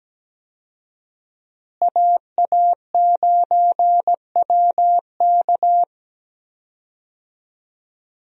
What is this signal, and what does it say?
Morse code, heard as a keyed tone, about 17 wpm, 700 Hz. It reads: AA9WK
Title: USA Call Signs - 5 Characters - Repeated 3x - 17wpm